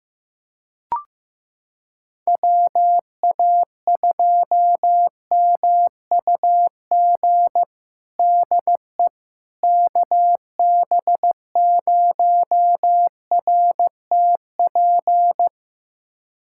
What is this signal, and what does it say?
Morse code, heard as a keyed tone, about 15 wpm, 700 Hz. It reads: WA2MUG DE KB0RTP
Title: USA Callsign DE Callsign - 6 Characters - 15wpm